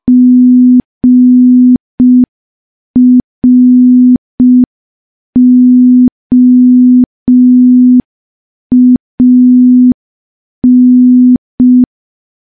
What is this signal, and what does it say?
Morse code, heard as a keyed tone, about 5 wpm, 250 Hz. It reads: GROAN
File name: morse / morse-training-05WPM